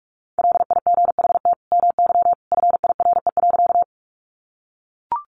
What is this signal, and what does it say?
Morse code, heard as a keyed tone, about 44 wpm, 700 Hz. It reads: LIGHT MY FIRE.
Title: Sentences from Top 400 Words 45wpm